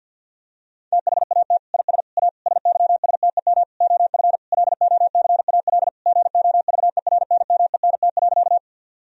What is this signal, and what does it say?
Morse code, heard as a keyed tone, about 50 wpm, 700 Hz. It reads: THAT IS A SYSTEM OF LOCAL GOVERNMENT.
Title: Sentences from Top 1000 Words - Review Part 3 - 50wpm